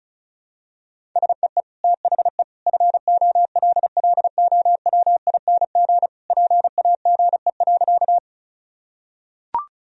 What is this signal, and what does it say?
Morse code, heard as a keyed tone, about 34 wpm, 700 Hz. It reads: SEE THE FOLLOWING PAGE.
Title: Sentences from Top 1000 Words - Review Part 1 - 35wpm